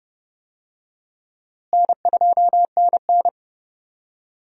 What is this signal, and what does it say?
Morse code, heard as a keyed tone, about 30 wpm, 700 Hz. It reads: N2NN
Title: USA Call Signs - 4 Characters - Repeated 3x - 30wpm